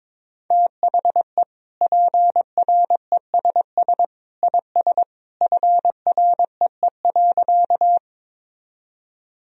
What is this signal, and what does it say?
Morse code, heard as a keyed tone, about 22 wpm, 700 Hz. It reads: THE PRESS IS FREE.
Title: Sentences from Top 1000 Words - Review Part 1 - 22wpm